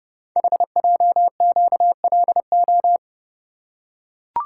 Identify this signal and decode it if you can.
Morse code, heard as a keyed tone, about 30 wpm, 700 Hz. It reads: HJQLO
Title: Code Groups - 5 Letters - 30wpm